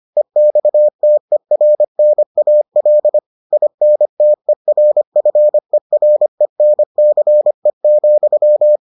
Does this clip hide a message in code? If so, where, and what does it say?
Yes: Morse code from 0.2 to 8.8 s, heard as a keyed tone, about 25 wpm, 600 Hz. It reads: EXTERNAL INTERFERENCE,